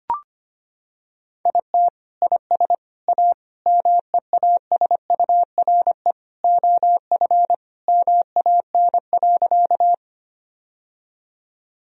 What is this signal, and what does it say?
Morse code, heard as a keyed tone, about 25 wpm, 700 Hz. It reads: IT IS A MEASURE OF MAN.
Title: Sentences from Top 1000 Words - Review Part 2 - 25wpm